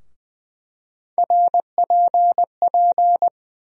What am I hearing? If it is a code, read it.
Morse code, heard as a keyed tone, about 20 wpm, 700 Hz. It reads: RPP
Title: USA Call Sign Postfixes - 3 Letters 20wpm